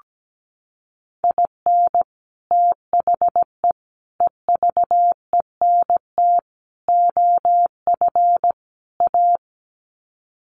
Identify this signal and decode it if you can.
Morse code, heard as a keyed tone, about 17 wpm, 700 Hz. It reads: IN THE EVENT OF A
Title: Sets of 5 Words Encore 17wpm